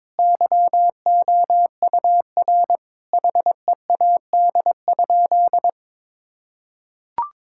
Morse code, heard as a keyed tone, about 22 wpm, 700 Hz. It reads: YOUR HEAD?
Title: Sentences from Top 1000 Words - Review Part 5 - 22wpm